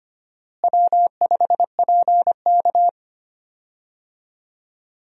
Morse code, heard as a keyed tone, about 25 wpm, 700 Hz. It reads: W5PK
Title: USA Call Signs - 4 Characters - Repeated 2x - 25wpm